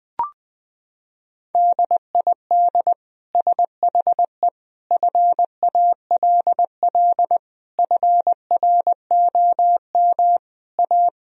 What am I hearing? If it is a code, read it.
Morse code, heard as a keyed tone, about 21 wpm, 700 Hz. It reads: DID SHE FALL FROM A
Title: Sentences from Top 1000 Words - Review Part 2 - 20wpm